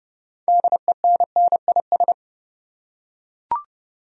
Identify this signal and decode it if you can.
Morse code, heard as a keyed tone, about 30 wpm, 700 Hz. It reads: DENNIS
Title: USA Common Names - 30wpm